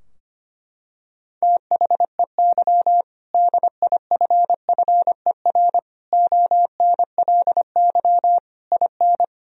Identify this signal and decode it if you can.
Morse code, heard as a keyed tone, about 25 wpm, 700 Hz. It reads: THEY DIFFER ONLY IN
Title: Sentences from Top 1000 Words - Review Part 2 - 25wpm